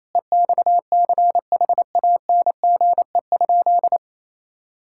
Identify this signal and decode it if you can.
Morse code, heard as a keyed tone, about 27 wpm, 700 Hz. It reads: EXCHANGE?
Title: Sentences from Top 1000 Words - Review Part 6 - 28wpm